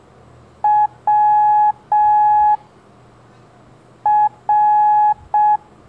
Morse code, heard as a keyed tone, about 6 wpm, 840 Hz. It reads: W R
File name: blog / LETRAS COM SOM SEMELHANTE MORSE